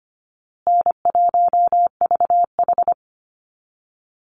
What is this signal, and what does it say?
Morse code, heard as a keyed tone, about 25 wpm, 700 Hz. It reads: N1VH
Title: Easy Call Signs Repeated 3x - 25wpm